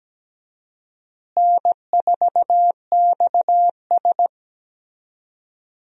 Morse code, heard as a keyed tone, about 17 wpm, 700 Hz. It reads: N4XS
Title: USA Call Signs - 4 Characters - Repeated 2x - 17wpm